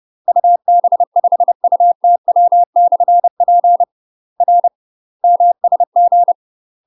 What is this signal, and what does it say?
Morse code, heard as a keyed tone, about 30 wpm, 700 Hz. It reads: UB5UTW/P R MSG
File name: UK MASTER QSOs TRAINING / 30WPM / UK MASTER QSOs 0039